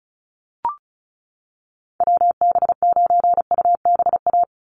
Morse code, heard as a keyed tone, about 34 wpm, 700 Hz. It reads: WB9UBA
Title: USA Call Signs 35wpm